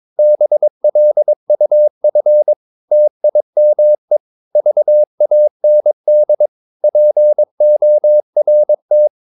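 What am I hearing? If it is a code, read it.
Morse code, heard as a keyed tone, about 22 wpm, 600 Hz. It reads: BLUF TIME VAND PORT
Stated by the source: Morse practice of short Danish words at 22 wpm, from CW med Gnister.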